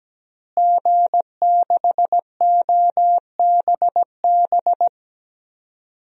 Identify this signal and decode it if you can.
Morse code, heard as a keyed tone, about 17 wpm, 700 Hz. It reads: G6OBB
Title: DXCC-100 Call Signs - 5 Characters - Repeated 2x - 17wpm